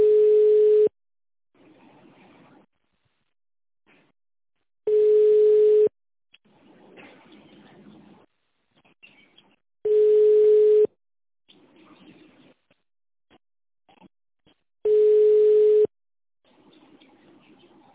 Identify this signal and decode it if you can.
Ringback tone (Europe, 425 Hz).